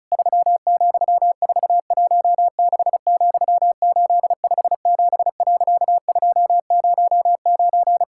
Morse code, heard as a keyed tone, about 34 wpm, 700 Hz. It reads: 3,416,857.209
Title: Pangram Challenge v2 - 35wpm